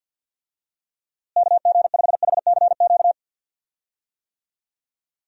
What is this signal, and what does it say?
Morse code, heard as a keyed tone, about 52 wpm, 700 Hz. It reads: KG5HCQ
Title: USA Call Signs - 6 Characters - Repeated 2x - 50wpm